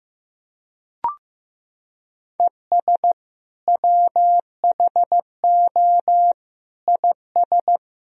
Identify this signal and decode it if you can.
Morse code, heard as a keyed tone, about 16 wpm, 700 Hz. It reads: ES WHO IS
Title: Sentences from Top 200 Words and Abbreviations 15wpm